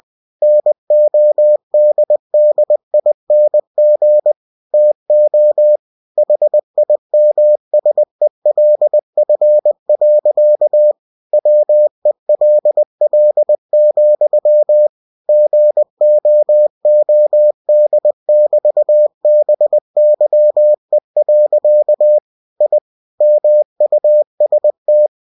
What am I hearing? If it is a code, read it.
Morse code, heard as a keyed tone, about 20 wpm, 600 Hz. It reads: NODDING TO HIMSELF. WELL, GOOD<BT>BYE. I MUST
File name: cw-books / winnie-the-pooh / winnie-20wpm-page-0016